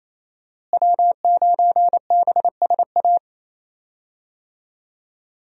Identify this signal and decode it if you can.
Morse code, heard as a keyed tone, about 27 wpm, 700 Hz. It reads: W9BSA
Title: USA Call Signs - 5 Characters - Repeated 2x - 28wpm